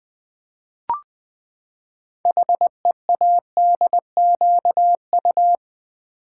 Morse code, heard as a keyed tone, about 20 wpm, 700 Hz. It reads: HEADQU
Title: Build A Word - 6 Letters At A Time - 12-Letter Words - 20wpm